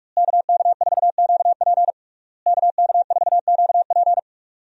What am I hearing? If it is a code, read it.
Morse code, heard as a keyed tone, about 46 wpm, 700 Hz. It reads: KK4QP KK4QP
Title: USA Call Signs - 5 Characters - Repeated 3x - 45wpm